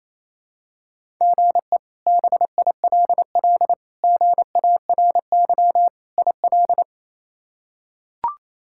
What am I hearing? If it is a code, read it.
Morse code, heard as a keyed tone, about 28 wpm, 700 Hz. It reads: GE BILL GARY IL
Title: K1USN SST QSO Practice - 28wpm